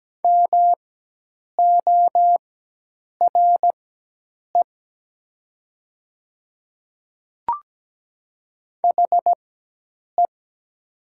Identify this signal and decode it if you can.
Morse code, heard as a keyed tone, about 17 wpm, spaced as for 8 wpm, 700 Hz. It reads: MORE HE
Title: 4x Character Spacing - Top 200 Words - 17wpm